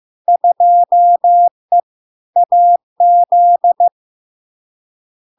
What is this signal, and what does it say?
Morse code, heard as a keyed tone, about 15 wpm, 700 Hz. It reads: 2E AZ
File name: LARK-FD-2022-Log-in-CW-15wpm-700-Hz